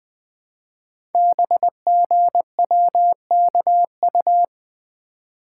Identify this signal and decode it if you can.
Morse code, heard as a keyed tone, about 20 wpm, 700 Hz. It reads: BGWKU
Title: Code Groups - 5 Letters - 20wpm